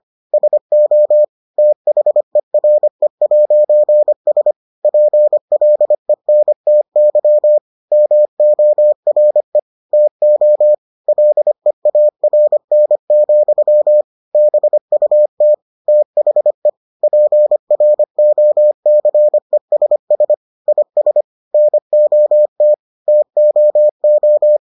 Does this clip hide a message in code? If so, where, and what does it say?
Yes: Morse code from 0.3 to 24.6 s, heard as a keyed tone, about 25 wpm, 600 Hz. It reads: SO THERE'S PLENTY MORE TO LEARN, BUT THE PROCESS IS NOT TOO